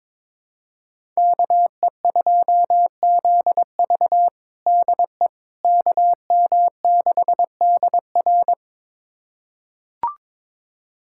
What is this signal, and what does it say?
Morse code, heard as a keyed tone, about 22 wpm, 700 Hz. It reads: KE2ZV DE KM6DR
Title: USA Callsign DE Callsign - 5 Characters - 22wpm